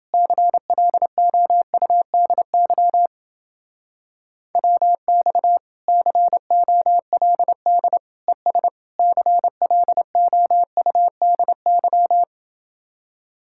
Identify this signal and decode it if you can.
Morse code, heard as a keyed tone, about 28 wpm, 700 Hz. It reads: CLOUDY WX COLD ES CLOUDY